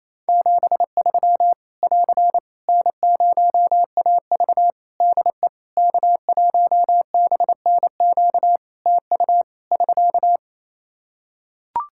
Morse code, heard as a keyed tone, about 27 wpm, 700 Hz. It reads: 73 <AR> N0AV DE K1BNQ TU <SK>
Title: Practice QSO - Exchange 4 - Ending - 28wpm